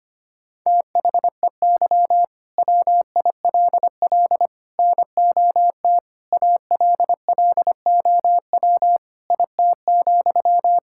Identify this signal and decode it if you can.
Morse code, heard as a keyed tone, about 25 wpm, 700 Hz. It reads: THEY WILL NOT ALLOW IT,
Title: Sentences from Top 1000 Words - Review Part 5 - 25wpm